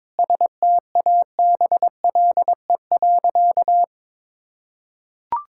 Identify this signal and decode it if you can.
Morse code, heard as a keyed tone, about 22 wpm, 700 Hz. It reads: STABLE.
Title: Sentences from Top 1600 Words 22wpm